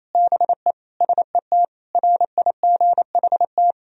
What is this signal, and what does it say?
Morse code, heard as a keyed tone, about 27 wpm, 700 Hz. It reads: BE SET RIGHT
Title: Sentences from Top 1000 Words - Review Part 1 - 28wpm